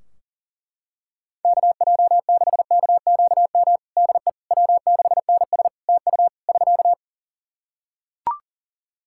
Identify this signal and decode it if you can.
Morse code, heard as a keyed tone, about 37 wpm, 700 Hz. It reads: KJ6KQM DE W6NS TU <SK>
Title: Practice QSO - Exchange 4 - Ending - 40wpm